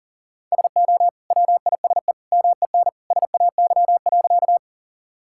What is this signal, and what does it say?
Morse code, heard as a keyed tone, about 40 wpm, 700 Hz. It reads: SO WISE MEN SAY.
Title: Sentences from Top 1600 Words 40wpm